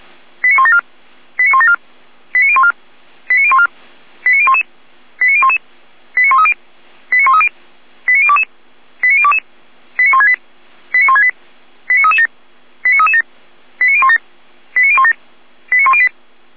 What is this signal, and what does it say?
Five-tone selective call (ZVEI), heard as quick runs of five tones, tones of about 70 ms. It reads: 89174 89174 89014 89014 89011 89011 89130 89130 89022 89022 89168 89168 89337 89337 89017 89017 89108
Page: SDR/ZVEI